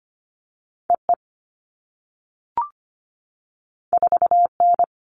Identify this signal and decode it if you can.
Morse code, heard as a keyed tone, about 25 wpm, 700 Hz. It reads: EE 4N
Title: All Cut Numbers - 2-Digit 25wpm